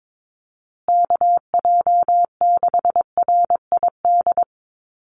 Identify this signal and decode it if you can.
Morse code, heard as a keyed tone, about 22 wpm, 700 Hz. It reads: KJ6RID
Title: USA Call Signs Repeated 3x 22wpm